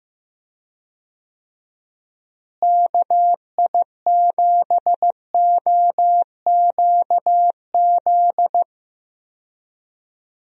Morse code, heard as a keyed tone, about 15 wpm, 700 Hz. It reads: KI7OQZ
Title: USA Call Signs - 6 Characters - Repeated 3x - 15wpm